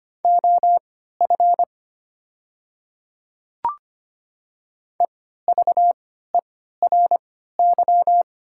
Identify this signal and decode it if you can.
Morse code, heard as a keyed tone, about 25 wpm, spaced as for 14 wpm, 700 Hz. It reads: OF EVERY
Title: 3x Character Spacing - Top 200 Words - 25wpm